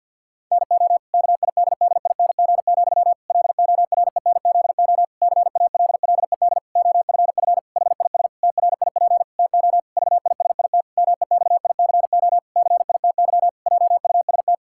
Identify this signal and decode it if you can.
Morse code, heard as a keyed tone, about 50 wpm, 700 Hz. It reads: NO KIDDING, LORENZO CALLED OFF HIS TRIP TO VISIT MEXICO CITY JUST
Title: Pangram Challenge v2 - 50wpm